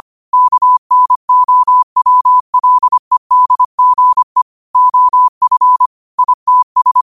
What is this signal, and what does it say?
Morse code, heard as a keyed tone, about 25 wpm, 1000 Hz. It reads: KNOWLEDGE OF ITS